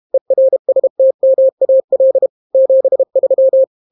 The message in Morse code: ERSTMAL 73